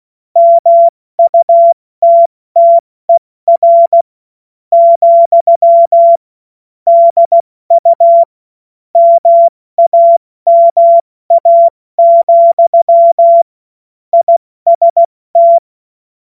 Morse code, heard as a keyed tone, about 16 wpm, spaced as for 14 wpm, 680 Hz. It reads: MUTTER, DU MAMA, IST
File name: Morsekurs / Witze / Witz 31. Januar 2023 70 F80